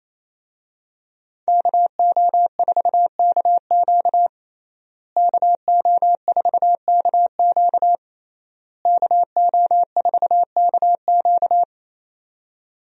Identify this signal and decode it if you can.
Morse code, heard as a keyed tone, about 28 wpm, 700 Hz. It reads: KO4KQ KO4KQ KO4KQ
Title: USA Call Signs - 5 Characters - Repeated 3x - 28wpm